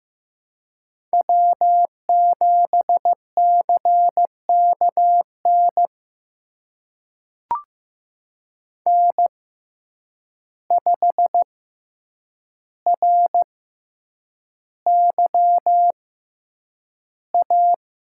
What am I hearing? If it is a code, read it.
Morse code, heard as a keyed tone, about 15 wpm, spaced as for 8 wpm, 700 Hz. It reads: W7CKN N5RYA
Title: USA Call Signs - 5 Characters - 6x Character Spacing - 15wpm